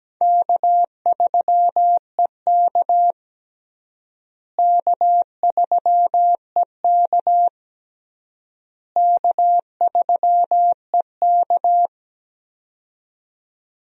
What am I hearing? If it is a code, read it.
Morse code, heard as a keyed tone, about 17 wpm, 700 Hz. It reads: K3EK K3EK K3EK
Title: USA Call Signs - 4 Characters - Repeated 3x - 17wpm